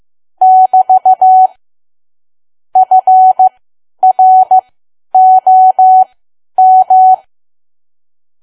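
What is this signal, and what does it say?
Morse code, heard as a keyed tone, about 15 wpm, spaced as for 10 wpm, 750 Hz. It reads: <BT> FROM
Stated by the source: Morse learning practice at 10 wpm.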